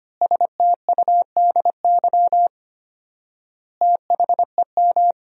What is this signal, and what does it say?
Morse code, heard as a keyed tone, about 25 wpm, 700 Hz. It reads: STUDY THEM
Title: 4x Word-Spacing - Sentences from Top 300 Words - 25wpm